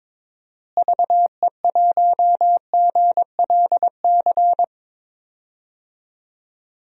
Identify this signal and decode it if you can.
Morse code, heard as a keyed tone, about 22 wpm, 700 Hz. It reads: VE1GLC
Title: DXCC-100 Call Signs - 6 Characters - Repeated 2x - 22wpm